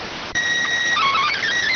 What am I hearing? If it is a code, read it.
SSTV (slow-scan television), PD 290.